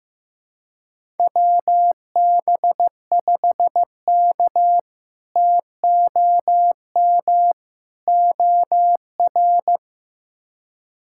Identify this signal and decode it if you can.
Morse code, heard as a keyed tone, about 15 wpm, 700 Hz. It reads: WB5K TOM OR
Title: K1USN SST QSO Practice - 15wpm